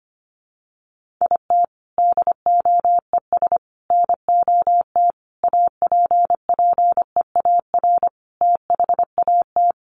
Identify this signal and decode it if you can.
Morse code, heard as a keyed tone, about 25 wpm, 700 Hz. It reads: IT DOES NOT APPEAR THAT
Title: Sentences from Top 1400 Words 25wpm